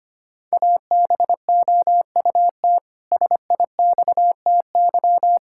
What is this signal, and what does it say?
Morse code, heard as a keyed tone, about 25 wpm, 700 Hz. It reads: ABOUT SIXTY
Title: Pangram Challenge v2 - 25wpm